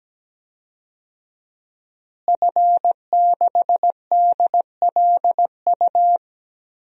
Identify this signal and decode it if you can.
Morse code, heard as a keyed tone, about 17 wpm, 700 Hz. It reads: F6DLU
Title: DXCC-100 Call Signs - 5 Characters - Repeated 2x - 17wpm